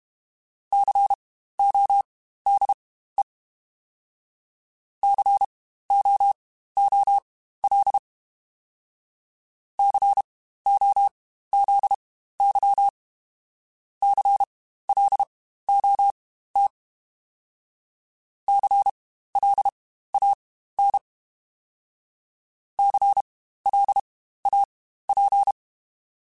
Morse code, heard as a keyed tone, about 32 wpm, spaced as for 15 wpm, 800 Hz. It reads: CODE COOL COZY CLOT CLAN CLAP
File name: Spaced Alphas / Spaced Alphas mp3s / Cs sp @30